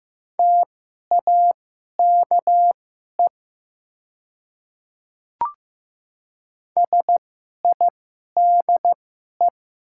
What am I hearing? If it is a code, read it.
Morse code, heard as a keyed tone, about 15 wpm, spaced as for 11 wpm, 700 Hz. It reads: TAKE SIDE
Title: 2x Character Spacing - Top 200 Words - 15wpm